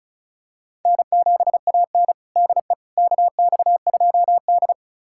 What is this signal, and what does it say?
Morse code, heard as a keyed tone, about 36 wpm, 700 Hz. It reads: N7AN DE KX2D
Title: USA Callsign DE Callsign - 4 Characters - 35wpm